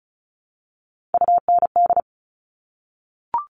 Morse code, heard as a keyed tone, about 36 wpm, 700 Hz. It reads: UND
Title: Top Three Letter Combinations in Words 35wpm